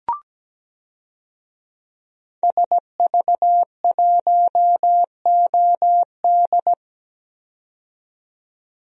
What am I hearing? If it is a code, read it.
Morse code, heard as a keyed tone, about 17 wpm, 700 Hz. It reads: SV1OD
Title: DXCC-100 Call Signs - 5 Characters - 17wpm